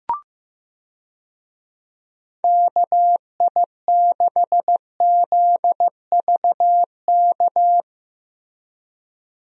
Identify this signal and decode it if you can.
Morse code, heard as a keyed tone, about 16 wpm, 700 Hz. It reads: KI6ZVK
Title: USA Call Signs - 6 Characters - Repeated 3x - 15wpm